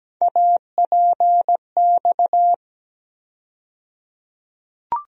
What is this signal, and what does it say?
Morse code, heard as a keyed tone, about 17 wpm, 700 Hz. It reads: APX
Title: USA Call Sign Postfixes - 3 Letters 17wpm